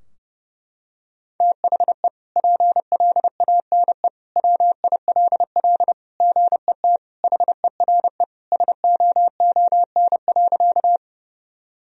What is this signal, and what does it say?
Morse code, heard as a keyed tone, about 32 wpm, 700 Hz. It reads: THE PLANE WILL GET HERE SOON.
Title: Sentences from Top 1000 Words - Review Part 3 - 30wpm